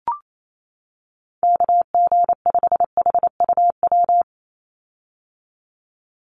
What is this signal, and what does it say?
Morse code, heard as a keyed tone, about 28 wpm, 700 Hz. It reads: KG5HUW
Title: USA Call Signs 28wpm